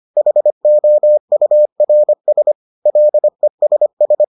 Morse code, heard as a keyed tone, about 25 wpm, 600 Hz. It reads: HOURS LESS